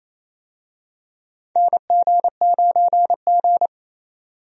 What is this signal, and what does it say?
Morse code, heard as a keyed tone, about 28 wpm, 700 Hz. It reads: NG9G
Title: USA Call Signs - 4 Characters - 28wpm